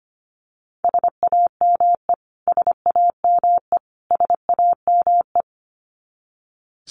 Morse code, heard as a keyed tone, about 25 wpm, 700 Hz. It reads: SAME SAME SAME
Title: Top 200 Words Repeated 25wpm